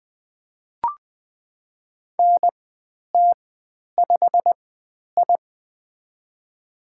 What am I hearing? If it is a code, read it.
Morse code, heard as a keyed tone, about 20 wpm, spaced as for 10 wpm, 700 Hz. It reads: NT5I
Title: 095 Lesson 16g - Letter c - Call Signs 20wpm